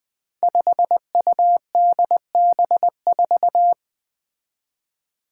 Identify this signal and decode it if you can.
Morse code, heard as a keyed tone, about 20 wpm, 700 Hz. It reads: 5UDB4
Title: Five Challenging Letter-Number Combinations 20wpm